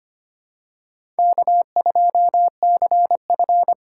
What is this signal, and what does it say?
Morse code, heard as a keyed tone, about 25 wpm, 700 Hz. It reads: K2CF
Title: USA Call Signs - 8x Character Spacing - 25wpm